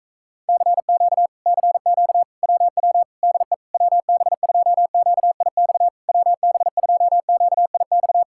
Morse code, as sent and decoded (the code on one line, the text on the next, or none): -.-. --.- / -.-. --.- / .-- .-- / -.. / .-- -... ..--- --.- .. -..- / .-- -... ..--- --.- .. -..-
CQ CQ WW DE WB2QIX WB2QIX